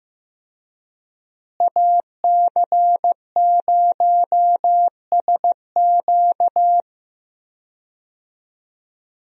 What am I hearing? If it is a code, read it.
Morse code, heard as a keyed tone, about 15 wpm, 700 Hz. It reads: AC0SQ
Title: USA Call Signs - 5 Characters - Repeated 3x - 15wpm